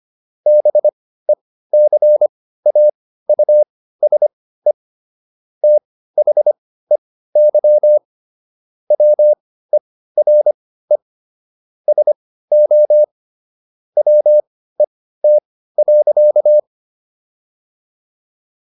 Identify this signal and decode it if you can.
Morse code, heard as a keyed tone, about 25 wpm, spaced as for 15 wpm, 600 Hz. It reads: BECAUSE THEY WERE SO WET.